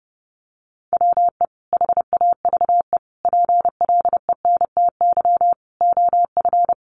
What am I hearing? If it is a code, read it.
Morse code, heard as a keyed tone, about 30 wpm, 700 Hz. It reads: WE HAVE PLENTY OF
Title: Sentences from Top 1300 Words 30wpm